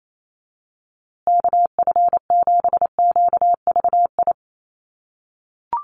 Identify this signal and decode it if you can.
Morse code, heard as a keyed tone, about 28 wpm, 700 Hz. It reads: KF7QVI
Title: USA Call Signs Repeated 3x 28wpm